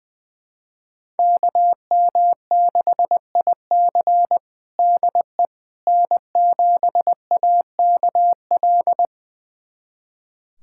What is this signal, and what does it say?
Morse code, heard as a keyed tone, about 20 wpm, 700 Hz. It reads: KM6IC DE N7AKL
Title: USA Callsign DE Callsign - 5 Characters - 20wpm